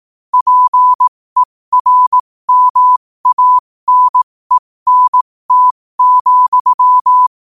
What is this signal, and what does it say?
Morse code, heard as a keyed tone, about 18 wpm, spaced as for 15 wpm, 1000 Hz. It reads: PERMANENT,
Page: qotd-at-15-wpm